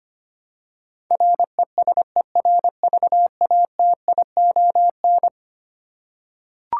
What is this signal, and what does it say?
Morse code, heard as a keyed tone, about 25 wpm, 700 Hz. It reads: RESERVATION
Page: morse-code-ninja-practice-25wpm